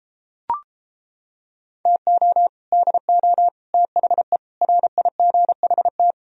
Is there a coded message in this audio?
Yes: Morse code from 1.9 to 6.1 s, heard as a keyed tone, about 29 wpm, 700 Hz. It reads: TO DO THE RIGHT